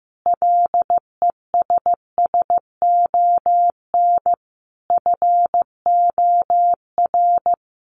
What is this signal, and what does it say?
Morse code, heard as a keyed tone, about 14 wpm, 700 Hz. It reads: LESSON FOR
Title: Sentences from Top 1200 Words 15wpm